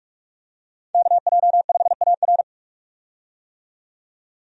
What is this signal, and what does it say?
Morse code, heard as a keyed tone, about 46 wpm, 700 Hz. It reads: KJ5AR
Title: USA Call Signs - 5 Characters - 45wpm